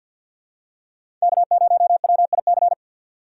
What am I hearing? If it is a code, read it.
Morse code, heard as a keyed tone, about 53 wpm, 700 Hz. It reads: K0WIC
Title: USA Call Signs - 5 Characters - 8x Character Spacing - 50wpm